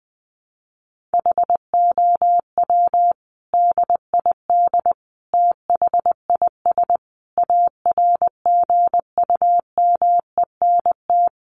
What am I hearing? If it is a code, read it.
Morse code, heard as a keyed tone, about 20 wpm, 700 Hz. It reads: HOW DID THIS ARGUMENT
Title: Sentences from Top 1400 Words 20wpm